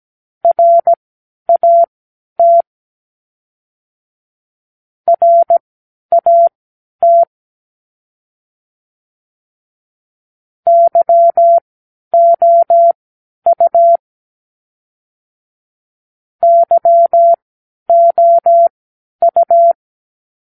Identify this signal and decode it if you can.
Morse code, heard as a keyed tone, about 17 wpm, spaced as for 10 wpm, 680 Hz. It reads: RAT RAT YOU YOU
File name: k7qo / Practicewords